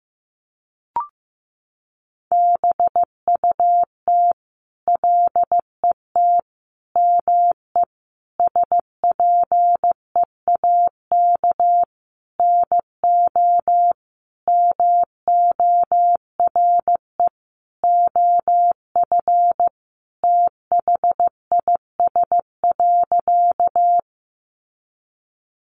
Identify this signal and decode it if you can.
Morse code, heard as a keyed tone, about 14 wpm, 700 Hz. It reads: BUT LET ME SPEAK NO MORE OF THIS.
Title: Sentences from Top 1100 Words 15wpm